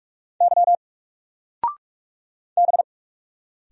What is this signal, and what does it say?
Morse code, heard as a keyed tone, about 46 wpm, 700 Hz. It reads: Y B
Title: Single Letter-Number 45wpm